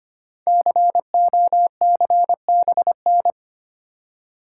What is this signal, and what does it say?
Morse code, heard as a keyed tone, about 25 wpm, 700 Hz. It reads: COCBN
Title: Code Groups - 5 Letters - 25wpm